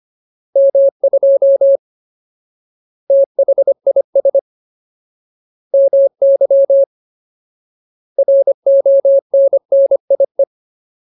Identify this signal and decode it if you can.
Morse code, heard as a keyed tone, about 25 wpm, 550 Hz. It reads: M2 THIS MY RONNIE
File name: Ultimate Word List / 25WPM Ultimate